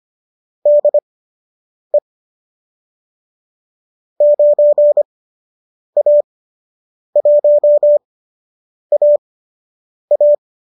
Morse code, heard as a keyed tone, about 25 wpm, spaced as for 8 wpm, 590 Hz. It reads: DE 9A1AA